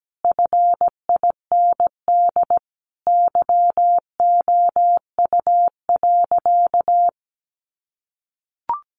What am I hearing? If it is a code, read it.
Morse code, heard as a keyed tone, about 17 wpm, 700 Hz. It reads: FIND YOU.
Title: Sentences from Top 1300 Words 17wpm